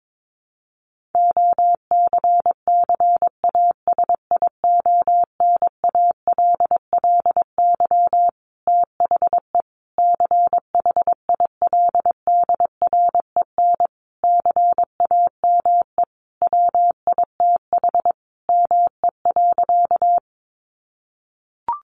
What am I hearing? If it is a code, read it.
Morse code, heard as a keyed tone, about 22 wpm, 700 Hz. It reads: OCCASIONALLY THE CHILDREN CAME WITH ME.
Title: Sentences from Top 1300 Words 22wpm